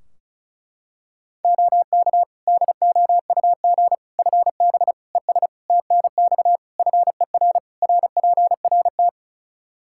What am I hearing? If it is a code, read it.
Morse code, heard as a keyed tone, about 36 wpm, 700 Hz. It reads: OK DOUG FB ES TNX FER RPRT
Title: Practice QSO - Exchange 2 - Info - 35wpm